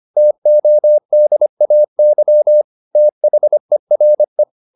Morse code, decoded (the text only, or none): TODAY THERE